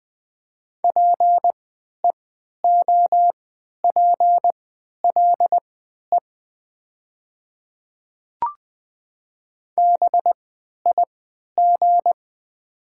Morse code, heard as a keyed tone, about 20 wpm, spaced as for 11 wpm, 690 Hz. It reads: PEOPLE BIG
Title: 3x Character Spacing - Top 200 Words - 20wpm